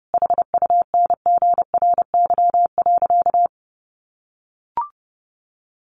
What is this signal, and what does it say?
Morse code, heard as a keyed tone, about 30 wpm, 700 Hz. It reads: HUNGRY.
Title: A Sound Of Thunder 30wpm